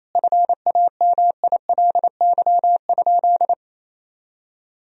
Morse code, heard as a keyed tone, about 28 wpm, 700 Hz. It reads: FAMILY?
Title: Sentences from Top 1000 Words - Review Part 4 - 28wpm